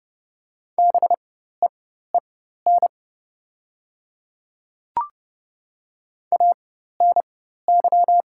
Morse code, heard as a keyed tone, about 30 wpm, spaced as for 14 wpm, 700 Hz. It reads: BEEN ANY
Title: 4x Character Spacing - Top 500 Words - 30wpm